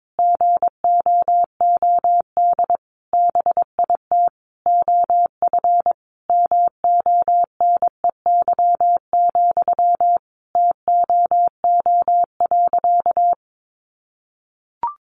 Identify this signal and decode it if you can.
Morse code, heard as a keyed tone, about 22 wpm, 700 Hz. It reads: GOOD BIT OF MONEY, TOO.
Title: Sentences from Top 900 Words 22wpm